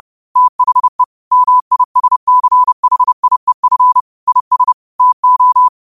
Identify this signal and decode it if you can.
Morse code, heard as a keyed tone, about 30 wpm, 1000 Hz. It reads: THE MISCHIEF IS TO
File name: qotdfiles / 30wpm